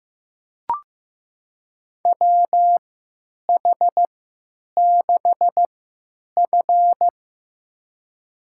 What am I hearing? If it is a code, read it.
Morse code, heard as a keyed tone, about 15 wpm, spaced as for 9 wpm, 700 Hz. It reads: WH6F
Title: USA Call Signs - 4 Characters - 3x Character Spacing - 15wpm